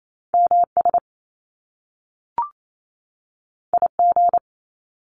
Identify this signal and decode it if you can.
Morse code, heard as a keyed tone, about 28 wpm, 700 Hz. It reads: MS IG
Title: Top Two Letter Combinations in Words 28wpm